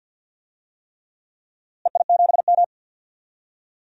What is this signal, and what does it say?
Morse code, heard as a keyed tone, about 48 wpm, 700 Hz. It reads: EI7M
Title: DXCC-100 Call Signs - 4 Characters - Repeated 3x - 50wpm